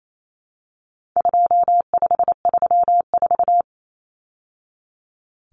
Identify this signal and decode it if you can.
Morse code, heard as a keyed tone, about 28 wpm, 700 Hz. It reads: 2534
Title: Numbers - 4-Digit 28wpm